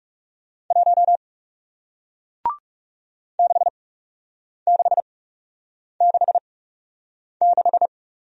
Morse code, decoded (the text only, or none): 1 6666